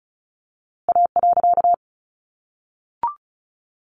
Morse code, heard as a keyed tone, about 34 wpm, 700 Hz. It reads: A.
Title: A Sound Of Thunder 35wpm